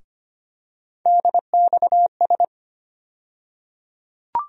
Morse code, heard as a keyed tone, about 25 wpm, 700 Hz. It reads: DXS
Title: USA Call Sign Postfixes - 3 Letters 25wpm